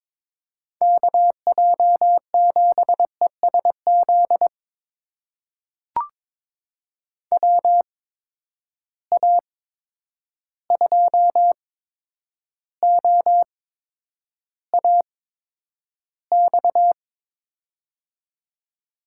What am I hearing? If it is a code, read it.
Morse code, heard as a keyed tone, about 22 wpm, spaced as for 9 wpm, 700 Hz. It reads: KJ7ESZ WA2OAX
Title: USA Call Signs - 6 Characters - 8x Character Spacing - 22wpm